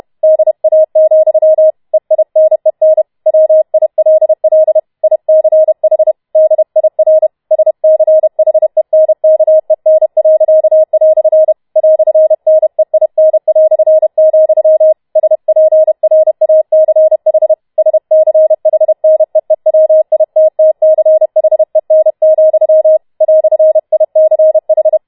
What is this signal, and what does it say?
Morse code, heard as a keyed tone, about 31 wpm, 630 Hz. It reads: DA, EINEN WILL ICH DIR SCHENKEN." "NEIN", SPRACH SCHNEEWITTCHEN, "ICH